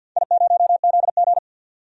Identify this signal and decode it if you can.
Morse code, heard as a keyed tone, about 53 wpm, 700 Hz. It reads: I0ZG
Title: DXCC-100 Call Signs - 4 Characters - 50wpm